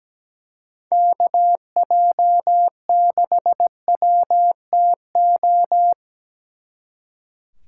Morse code, heard as a keyed tone, about 20 wpm, 700 Hz. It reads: KJ6WTO
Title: Speed Racing - USA Call Signs 17wpm